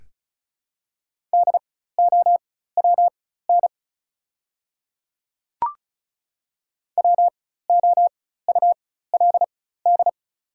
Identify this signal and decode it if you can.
Morse code, heard as a keyed tone, about 36 wpm, spaced as for 16 wpm, 700 Hz. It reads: DOWN WOULD
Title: 4x Character Spacing - Top 200 Words - 35wpm